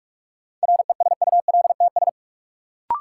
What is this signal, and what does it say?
Morse code, heard as a keyed tone, about 46 wpm, 700 Hz. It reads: RESULTS
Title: Sets of 4 Words - New Word Drill - 45wpm